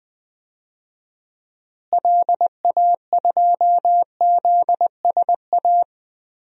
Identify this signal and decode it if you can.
Morse code, heard as a keyed tone, about 20 wpm, 700 Hz. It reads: LA2ZSA
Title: DXCC-100 Call Signs - 6 Characters - Repeated 2x - 20wpm